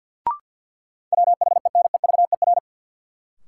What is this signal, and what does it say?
Morse code, heard as a keyed tone, about 52 wpm, 700 Hz. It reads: WHENEVER
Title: Sentences from Top 1200 Words - New Word Drill Alt - 50wpm